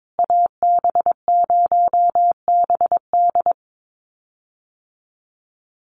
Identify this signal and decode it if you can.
Morse code, heard as a keyed tone, about 22 wpm, 700 Hz. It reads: AB0BD
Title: USA Call Signs Repeated 2x 22wpm